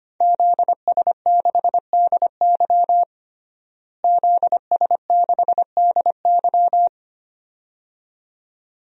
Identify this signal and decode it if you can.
Morse code, heard as a keyed tone, about 25 wpm, 700 Hz. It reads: ZS6DY ZS6DY
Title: DXCC-100 Call Signs - 5 Characters - Repeated 2x - 25wpm